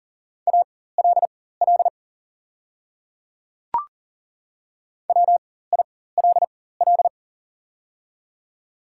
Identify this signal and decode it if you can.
Morse code, heard as a keyed tone, about 40 wpm, spaced as for 19 wpm, 700 Hz. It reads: ALL WILL
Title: 4x Character Spacing - Top 100 Words - 40wpm